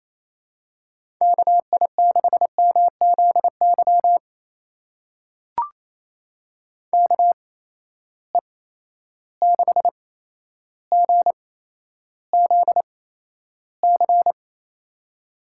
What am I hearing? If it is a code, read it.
Morse code, heard as a keyed tone, about 28 wpm, spaced as for 12 wpm, 700 Hz. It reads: KI6MZY KE6GZC
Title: USA Call Signs - 6 Characters - 8x Character Spacing - 28wpm